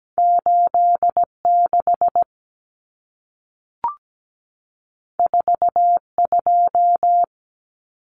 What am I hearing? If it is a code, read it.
Morse code, heard as a keyed tone, about 17 wpm, 700 Hz. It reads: 86 42
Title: Common Cut Numbers - 2-Digit 17wpm